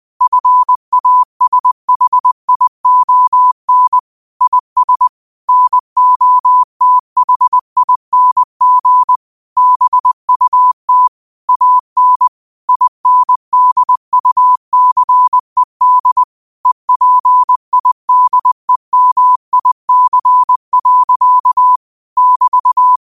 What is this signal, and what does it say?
Morse code, heard as a keyed tone, about 20 wpm, 1000 Hz. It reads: FASHION IS NOTHING BUT AN INDUCED EPIDEMIC. <BT>
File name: qotdfiles / 20wpm